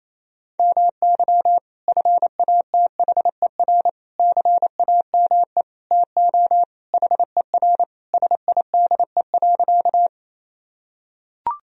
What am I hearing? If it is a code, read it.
Morse code, heard as a keyed tone, about 27 wpm, 700 Hz. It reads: MY FATHER CAME TO HER SIDE.
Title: Sentences from Top 1000 Words - Review Part 1 - 28wpm